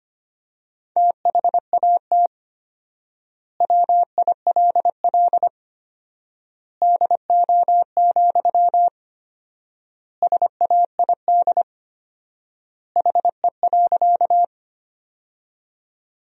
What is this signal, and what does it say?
Morse code, heard as a keyed tone, about 25 wpm, 700 Hz. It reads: THAT WILL DO, SAID HE.
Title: 4x Word-Spacing - Sentences from Top 100 Words - 25wpm